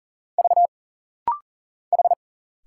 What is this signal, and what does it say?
Morse code, heard as a keyed tone, about 40 wpm, 700 Hz. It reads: V H
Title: Single Letter-Number Alt - Rapid-Fire 40wpm